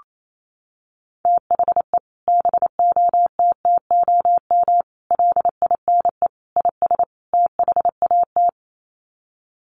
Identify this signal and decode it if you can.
Morse code, heard as a keyed tone, about 28 wpm, 700 Hz. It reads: THE BOTTOM LINE IS THAT
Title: Sets of 5 Words Encore 28wpm